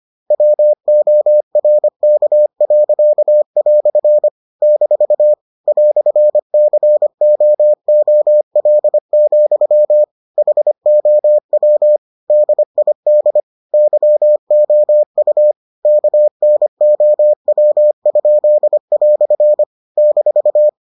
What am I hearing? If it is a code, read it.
Morse code, heard as a keyed tone, about 25 wpm, 600 Hz. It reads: WORK." - "COOL, HOW DID YOU KNOW?" -